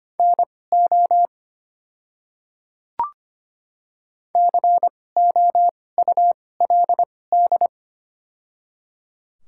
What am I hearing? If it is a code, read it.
Morse code, heard as a keyed tone, about 25 wpm, spaced as for 18 wpm, 690 Hz. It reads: NO COULD
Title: 2x Character Spacing - Top 100 Words - 25wpm